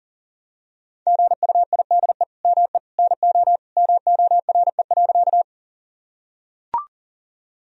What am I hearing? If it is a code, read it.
Morse code, heard as a keyed tone, about 40 wpm, 700 Hz. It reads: GUIDE ME NO MORE.
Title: Sentences from Top 1000 Words - Review Part 3 - 40wpm